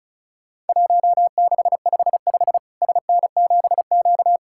Morse code, heard as a keyed tone, about 34 wpm, 700 Hz. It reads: 1655 SN7Q
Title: Worked All Europe DX Contest - QTC Exchange - 35wpm